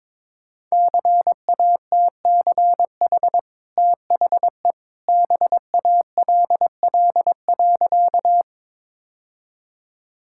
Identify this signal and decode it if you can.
Morse code, heard as a keyed tone, about 22 wpm, 700 Hz. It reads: CATCH THE BALL.
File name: practice / 22wpm / Easy Sentences with Hints - 22wpm